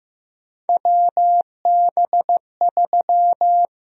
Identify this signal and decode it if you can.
Morse code, heard as a keyed tone, about 15 wpm, 700 Hz. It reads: WB3
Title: USA Call Sign Prefixes - Repeated 2x - 15wpm